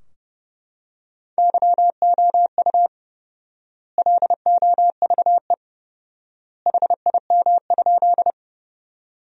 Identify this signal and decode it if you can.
Morse code, heard as a keyed tone, about 30 wpm, 700 Hz. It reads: YOU LOVE HIM?
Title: 4x Word-Spacing - Sentences from Top 400 Words - 30wpm